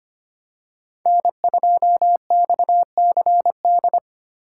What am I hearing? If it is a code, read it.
Morse code, heard as a keyed tone, about 25 wpm, 700 Hz. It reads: N2XCD
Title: USA Call Signs - 2x Character Spacing - 25wpm